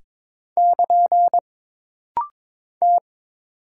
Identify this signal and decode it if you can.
Morse code, heard as a keyed tone, about 22 wpm, 700 Hz. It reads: <KN> T
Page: This rapid-fire timing is geared to push your ICR (Instant Character Recognition) skills.